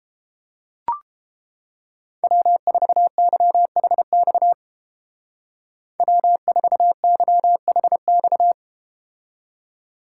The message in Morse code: W4YHX W4YHX

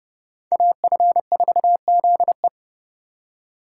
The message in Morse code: AF4ZE

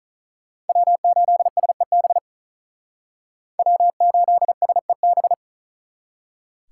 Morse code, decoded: W8SEB W8SEB

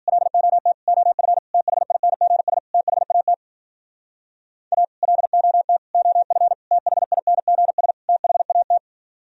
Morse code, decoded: LOT OF THINGS THAT A LOT OF THINGS THAT

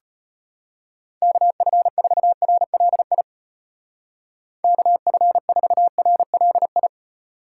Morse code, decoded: KF4RLI KF4RLI

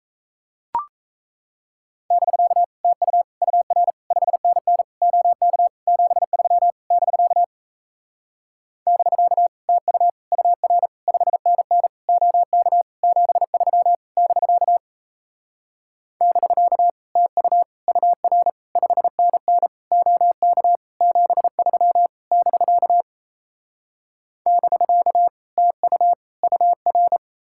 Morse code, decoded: <BK> TU UR 5NN OK 73 <BK> <BK> TU UR 5NN OK 73 <BK> <BK> TU UR 5NN OK 73 <BK> <BK> TU UR

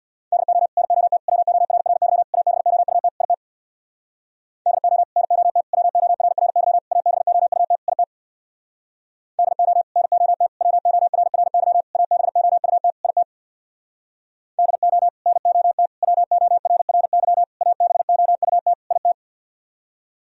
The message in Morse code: DO NOT WORRY ABOUT IT DO NOT WORRY ABOUT IT DO NOT WORRY ABOUT IT DO NOT WORRY ABOUT IT

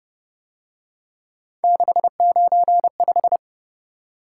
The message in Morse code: 695

